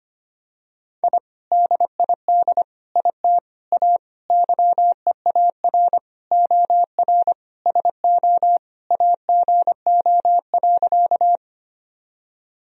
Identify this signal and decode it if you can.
Morse code, heard as a keyed tone, about 25 wpm, 700 Hz. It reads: I DID IT A YEAR OR SO AGO.
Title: Sentences from Top 1000 Words - Review Part 3 - 25wpm